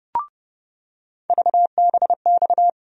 Morse code, heard as a keyed tone, about 30 wpm, 700 Hz. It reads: VBX
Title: USA Call Sign Postfixes - 3 Letters 30wpm